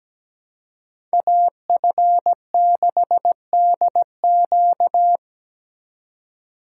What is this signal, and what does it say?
Morse code, heard as a keyed tone, about 20 wpm, 700 Hz. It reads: AF6DQ
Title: Speed Racing - USA 2x2 Call Signs 17wpm